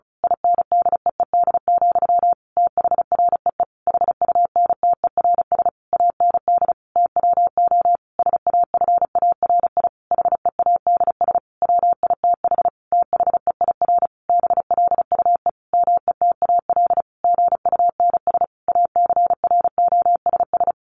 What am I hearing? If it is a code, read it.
Morse code, heard as a keyed tone, about 34 wpm, 700 Hz. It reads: INDEED, THREE HUNTERS AND TWO SAFARI HEADS WITH THEIR BLUE METAL GUNS ACROSS
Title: A Sound Of Thunder 35wpm